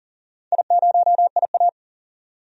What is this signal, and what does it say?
Morse code, heard as a keyed tone, about 40 wpm, 700 Hz. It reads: I0IA
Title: DXCC-100 Call Signs - 4 Characters - 40wpm